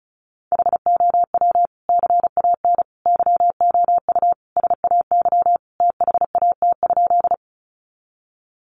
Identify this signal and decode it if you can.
Morse code, heard as a keyed tone, about 34 wpm, 700 Hz. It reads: HOW CAN YOU SAY THAT?
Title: Sentences from Top 200 Words 35wpm